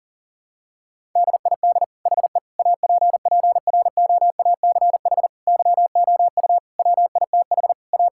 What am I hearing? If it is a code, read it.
Morse code, heard as a keyed tone, about 40 wpm, 700 Hz. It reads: DID HE APPROACH YOU WITH A
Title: Sentences from Top 1000 Words - Review Part 6 - 40wpm